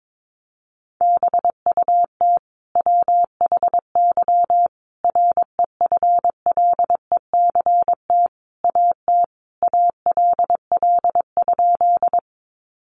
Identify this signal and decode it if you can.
Morse code, heard as a keyed tone, about 22 wpm, 700 Hz. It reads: BUT WHY REFLECT AT ALL?
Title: Sentences from Top 1500 Words 22wpm